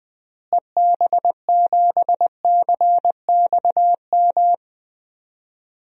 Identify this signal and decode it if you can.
Morse code, heard as a keyed tone, about 20 wpm, 700 Hz. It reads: EB7CXM
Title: DXCC-100 Call Signs - 6 Characters - Repeated 2x - 20wpm